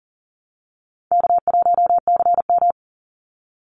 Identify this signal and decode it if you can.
Morse code, heard as a keyed tone, about 40 wpm, 700 Hz. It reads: K1CM
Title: NA SOTA Chaser Callsigns 40wpm